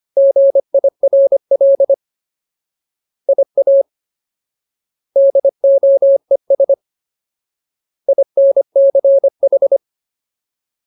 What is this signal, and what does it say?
Morse code, heard as a keyed tone, about 25 wpm, 550 Hz. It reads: GIRL IA DOES INCH
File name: Ultimate Word List / 25WPM Ultimate